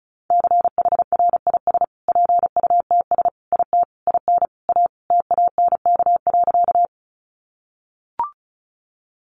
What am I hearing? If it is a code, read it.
Morse code, heard as a keyed tone, about 34 wpm, 700 Hz. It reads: CHRIS PUTS IT IN A TANK.
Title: Additional Easy Phrases 35wpm